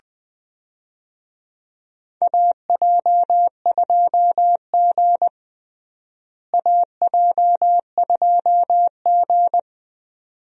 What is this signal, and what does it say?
Morse code, heard as a keyed tone, about 21 wpm, 700 Hz. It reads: AJ2G AJ2G
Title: USA Call Signs - 4 Characters - Repeated 2x - 20wpm